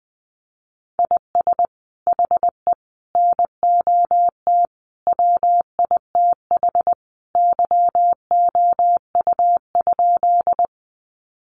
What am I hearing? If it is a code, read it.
Morse code, heard as a keyed tone, about 19 wpm, 700 Hz. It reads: IS HE NOT WITH YOU?
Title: Sentences from Top 600 Words 20wpm